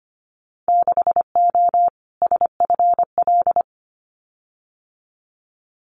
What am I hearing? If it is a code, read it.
Morse code, heard as a keyed tone, about 25 wpm, 690 Hz. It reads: 6O SFL
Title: Winter Field Day Exchanges Encore 25wpm